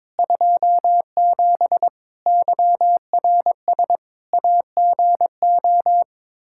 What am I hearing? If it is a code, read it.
Morse code, heard as a keyed tone, about 22 wpm, 700 Hz. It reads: 27 YRS AGO
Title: Practice QSO - Exchange 3 - More Info - 22wpm